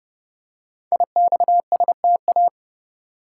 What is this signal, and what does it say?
Morse code, heard as a keyed tone, about 30 wpm, 690 Hz. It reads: IXSTA
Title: Code Groups - 5 Letters - 30wpm